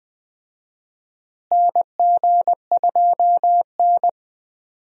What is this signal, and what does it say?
Morse code, heard as a keyed tone, about 20 wpm, 700 Hz. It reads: NG2N
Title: USA Call Signs - 4 Characters - Repeated 2x - 20wpm